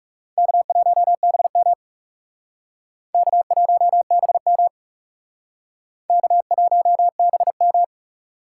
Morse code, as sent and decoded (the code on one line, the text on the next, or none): -.- .---- -... -- / -.- .---- -... -- / -.- .---- -... --
K1BM K1BM K1BM